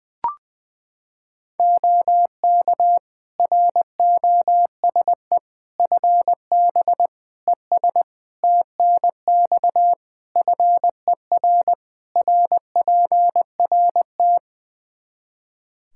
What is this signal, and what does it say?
Morse code, heard as a keyed tone, about 20 wpm, 700 Hz. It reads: OK ROSE FB ES TNX FER RPRT
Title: Practice QSO - Exchange 2 - Info - 20wpm